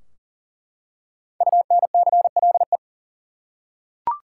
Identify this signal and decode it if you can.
Morse code, heard as a keyed tone, about 39 wpm, 700 Hz. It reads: UNCLE
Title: Five Letter Words - 40wpm